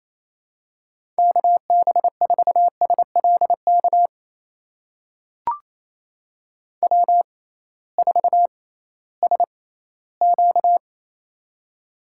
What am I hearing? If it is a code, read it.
Morse code, heard as a keyed tone, about 28 wpm, spaced as for 16 wpm, 700 Hz. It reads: KB4SLK W4SQ
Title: USA Call Signs - 6x Character Spacing - 28wpm